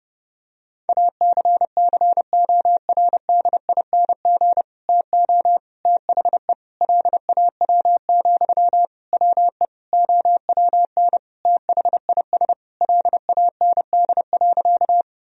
Morse code, heard as a keyed tone, about 30 wpm, 700 Hz. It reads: ACCORDING TO THE LAW, WE OWN THIS LAND.
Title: Sentences from Top 1000 Words - Review Part 4 - 30wpm